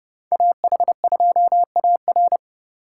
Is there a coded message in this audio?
Yes: Morse code from 0.3 to 2.4 s, heard as a keyed tone, about 38 wpm, 700 Hz. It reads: AH2AR